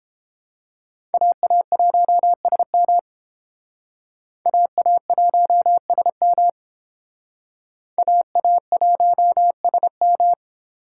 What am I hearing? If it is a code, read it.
Morse code, heard as a keyed tone, about 29 wpm, 700 Hz. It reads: AA1SM AA1SM AA1SM